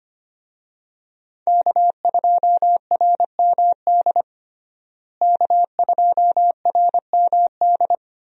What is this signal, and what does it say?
Morse code, heard as a keyed tone, about 25 wpm, 700 Hz. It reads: K2RMD K2RMD
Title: USA Call Signs - 5 Characters - Repeated 2x - 25wpm